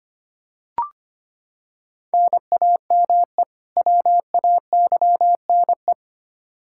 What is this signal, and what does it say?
Morse code, heard as a keyed tone, about 25 wpm, 700 Hz. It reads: NAME WAYNE
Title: Practice QSO - Exchange 1 - Beginning - 25wpm